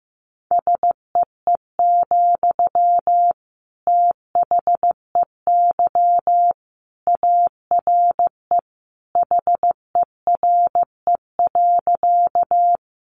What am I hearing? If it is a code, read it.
Morse code, heard as a keyed tone, about 15 wpm, 700 Hz. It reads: SEE, THEY ARE HERE.
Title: Sentences from Top 200 Words 15wpm